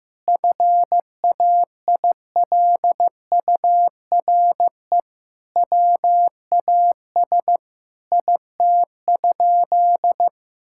Morse code, heard as a keyed tone, about 15 wpm, 700 Hz. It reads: FAILURE WAS IT?
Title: Sentences from Top 1000 Words - Review Part 6 - 15wpm